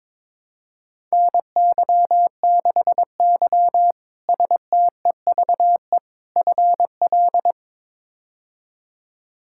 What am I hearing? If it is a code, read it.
Morse code, heard as a keyed tone, about 23 wpm, 700 Hz. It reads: NY6Y STEVE FL
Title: K1USN SST QSO Practice - 22wpm